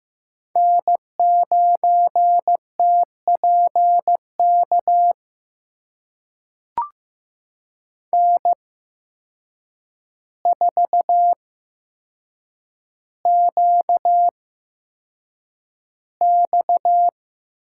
Morse code, heard as a keyed tone, about 15 wpm, spaced as for 7 wpm, 700 Hz. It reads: N9TPK N4QX
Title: USA Call Signs - 5 Characters - 8x Character Spacing - 15wpm